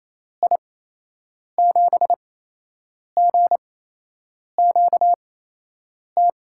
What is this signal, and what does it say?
Morse code, heard as a keyed tone, about 28 wpm, spaced as for 8 wpm, 700 Hz. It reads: I7GQT
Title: USA Call Signs - 6 Characters - 8x Character Spacing - 28wpm